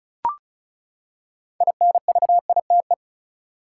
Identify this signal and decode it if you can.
Morse code, heard as a keyed tone, about 35 wpm, 700 Hz. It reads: INVITE
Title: Six Letter Words - 35wpm